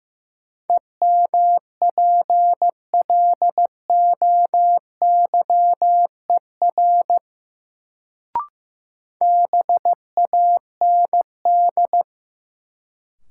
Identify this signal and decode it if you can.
Morse code, heard as a keyed tone, about 15 wpm, 700 Hz. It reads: EMPLOYER BAND
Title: Sentences from Top 1600 Words - New Word Drill Alt - 15wpm